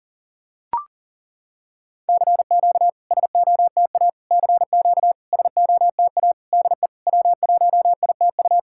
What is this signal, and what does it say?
Morse code, heard as a keyed tone, about 40 wpm, 690 Hz. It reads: CQ SOTA CQ SOTA DE W1ITU
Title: SOTA QSO Practice 40wpm